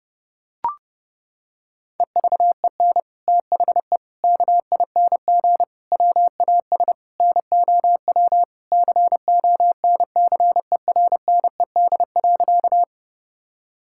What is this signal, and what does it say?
Morse code, heard as a keyed tone, about 30 wpm, 700 Hz. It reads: EVEN THE KING WAS NOW CONCERNED.
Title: Sentences from Top 1000 Words - Review Part 6 - 30wpm